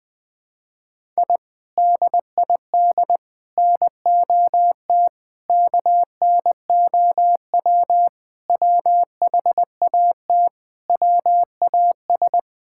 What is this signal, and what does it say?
Morse code, heard as a keyed tone, about 20 wpm, 700 Hz. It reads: I DID NOT KNOW WHAT WAS
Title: Sentences with 4-Word Phrases - Spoken Beginning - 20wpm